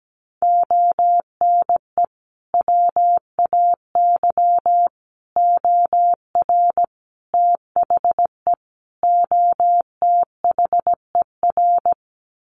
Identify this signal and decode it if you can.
Morse code, heard as a keyed tone, about 17 wpm, 700 Hz. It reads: ONE WAY OR THE OTHER
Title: Sets of 5 Words 17wpm